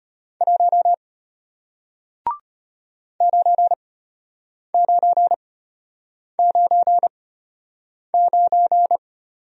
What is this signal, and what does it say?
Morse code, heard as a keyed tone, about 33 wpm, spaced as for 8 wpm, 700 Hz. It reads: 1 9999